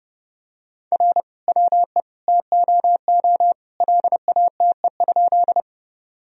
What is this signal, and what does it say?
Morse code, heard as a keyed tone, about 32 wpm, 700 Hz. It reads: R WE TOO LATE?
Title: Sentences from Top 300 Words and Abbreviations 30wpm